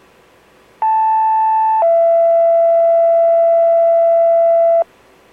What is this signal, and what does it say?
Two-tone page, heard as a one-second tone, then a three-second tone, about 880 then 670 Hz. It is heard from 0.8 to 4.8 s.